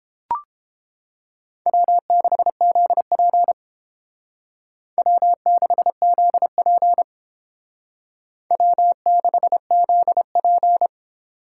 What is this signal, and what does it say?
Morse code, heard as a keyed tone, about 28 wpm, 690 Hz. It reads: W6ZP W6ZP W6ZP